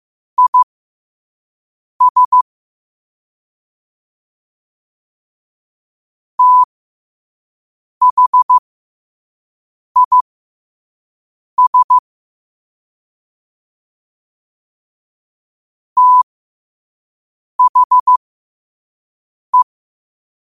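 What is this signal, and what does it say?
Morse code, heard as a keyed tone, about 15 wpm, spaced as for 5 wpm, 1000 Hz. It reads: IS THIS THE
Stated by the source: Morse quote of the day at 5 wpm.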